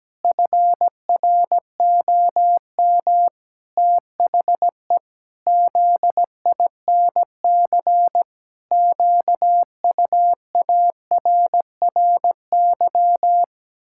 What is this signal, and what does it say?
Morse code, heard as a keyed tone, about 17 wpm, 700 Hz. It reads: FROM THE ZINC QUARRY
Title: Pangram Challenge v2 - 17wpm